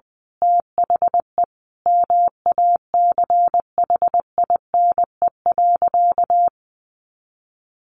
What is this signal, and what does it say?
Morse code, heard as a keyed tone, about 20 wpm, 700 Hz. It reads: THE MACHINE.
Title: A Sound Of Thunder 20wpm